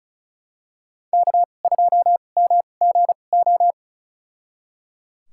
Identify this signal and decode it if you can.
Morse code, heard as a keyed tone, about 36 wpm, spaced as for 25 wpm, 700 Hz. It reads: K2MGO
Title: USA Call Signs - 5 Characters - 2x Character Spacing - 35wpm